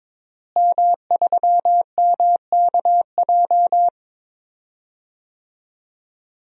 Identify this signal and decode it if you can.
Morse code, heard as a keyed tone, about 22 wpm, 700 Hz. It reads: M3MKJ
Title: DXCC-100 Call Signs - 5 Characters - Repeated 3x - 22wpm